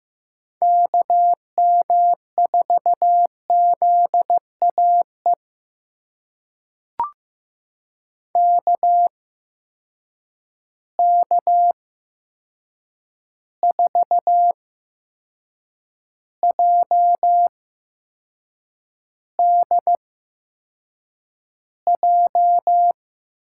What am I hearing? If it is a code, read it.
Morse code, heard as a keyed tone, about 15 wpm, spaced as for 6 wpm, 700 Hz. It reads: KM4ZAE KK4JDJ
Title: USA Call Signs - 8x Character Spacing - 15wpm